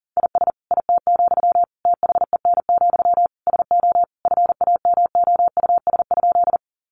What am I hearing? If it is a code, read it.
Morse code, heard as a keyed tone, about 40 wpm, 700 Hz. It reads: IS IT, THEN, SO FAMOUS?
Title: Sentences from Top 1500 Words 40wpm